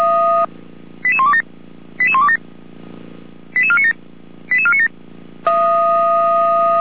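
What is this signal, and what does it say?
Five-tone selective call (ZVEI), heard as quick runs of five tones, tones of about 70 ms. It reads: 88127 88127 88497 88497
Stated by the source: ZVEI five-tone selective call 88129 received by SDR, from the Feuerwehr Orlamünde audio archive.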